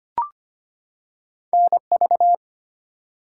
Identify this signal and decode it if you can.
Morse code, heard as a keyed tone, about 25 wpm, 700 Hz. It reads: NV
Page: morse-code-ninja-practice-25wpm